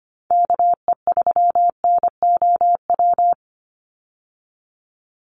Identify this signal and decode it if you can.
Morse code, heard as a keyed tone, about 25 wpm, 700 Hz. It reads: KE3NOW
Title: USA Call Signs 25wpm